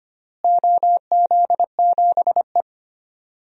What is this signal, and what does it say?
Morse code, heard as a keyed tone, about 25 wpm, 700 Hz. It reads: OZ7E
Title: DXCC-100 Call Signs - 4 Characters - Repeated 3x - 25wpm